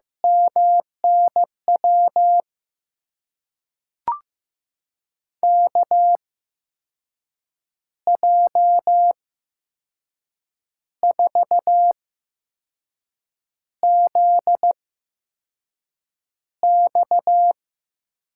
Morse code, heard as a keyed tone, about 15 wpm, spaced as for 5 wpm, 700 Hz. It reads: MNW KJ4ZX
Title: USA Call Signs - 6 Characters - 8x Character Spacing - 15wpm